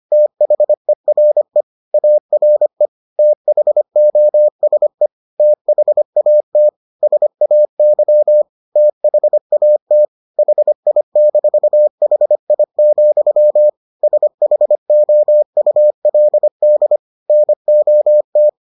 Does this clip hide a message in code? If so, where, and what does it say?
Yes: Morse code from 0.1 to 18.5 s, heard as a keyed tone, about 25 wpm, 600 Hz. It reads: THERE ARE THOSE THAT SAY THAT HI-HI, SHOULD NOT